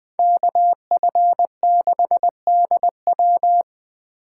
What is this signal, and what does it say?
Morse code, heard as a keyed tone, about 20 wpm, 700 Hz. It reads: KF6DW
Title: Speed Racing - USA 2x2 Call Signs 17wpm